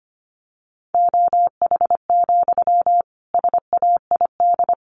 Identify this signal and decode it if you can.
Morse code, heard as a keyed tone, about 25 wpm, 700 Hz. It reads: OH, SAID
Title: Sentences from Top 500 Words 25wpm